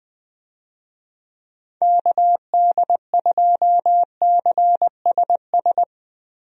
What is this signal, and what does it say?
Morse code, heard as a keyed tone, about 20 wpm, 700 Hz. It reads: KD2CSS
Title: USA Call Signs - 6 Characters - Repeated 2x - 20wpm